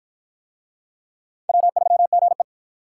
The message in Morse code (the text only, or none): W3GE